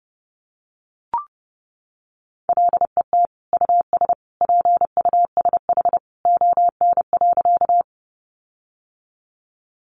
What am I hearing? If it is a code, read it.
Morse code, heard as a keyed tone, about 30 wpm, 700 Hz. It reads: LET US PUSH ON.
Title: Sentences from Top 1300 Words 30wpm